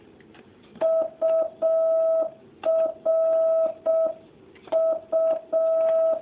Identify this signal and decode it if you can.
Morse code, heard as a keyed tone, about 6 wpm, 650 Hz. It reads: URU